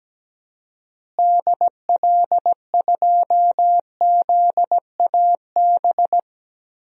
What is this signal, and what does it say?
Morse code, heard as a keyed tone, about 17 wpm, 700 Hz. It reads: DL2ZAB
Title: DXCC-100 Call Signs - 6 Characters - Repeated 3x - 17wpm